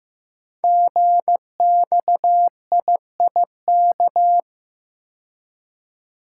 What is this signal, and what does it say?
Morse code, heard as a keyed tone, about 15 wpm, 700 Hz. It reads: GXIIK
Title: Code Groups - 5 Letters - 15wpm